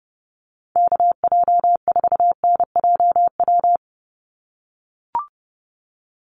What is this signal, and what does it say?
Morse code, heard as a keyed tone, about 30 wpm, 700 Hz. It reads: KJ4NJW
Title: USA Call Signs 30wpm